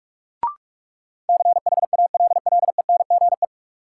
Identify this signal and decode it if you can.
Morse code, heard as a keyed tone, about 46 wpm, 700 Hz. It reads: CHALLENGE
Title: Sentences from Top 1300 Words - New Word Drill Alt - 45wpm